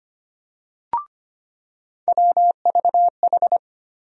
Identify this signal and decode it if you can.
Morse code, heard as a keyed tone, about 25 wpm, 700 Hz. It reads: WVH
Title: USA Call Sign Postfixes - 3 Letters 25wpm